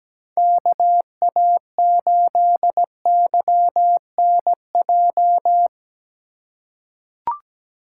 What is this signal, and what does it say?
Morse code, heard as a keyed tone, about 17 wpm, 700 Hz. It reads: KA8YNJ
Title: USA Call Signs - 6 Characters - 4x Character Spacing - 17wpm